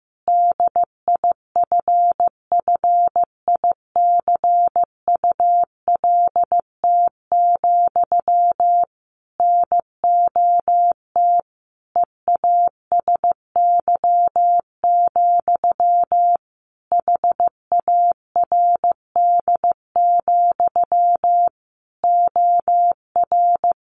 Morse code, heard as a keyed tone, about 15 wpm, 700 Hz. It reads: DIFFICULT, NOT EASY, HARD, OR
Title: Sentences from Top 1200 Words 15wpm